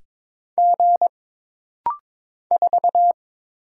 Morse code, heard as a keyed tone, about 22 wpm, 700 Hz. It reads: G 4
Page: This rapid-fire timing is geared to push your ICR (Instant Character Recognition) skills.